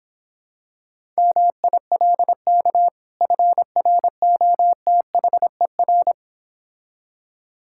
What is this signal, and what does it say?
Morse code, heard as a keyed tone, about 26 wpm, 700 Hz. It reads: MILK FROTHER